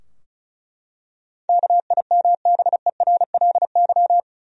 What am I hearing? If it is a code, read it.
Morse code, heard as a keyed tone, about 36 wpm, 700 Hz. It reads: KIMBERLY
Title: USA Common Names - 35wpm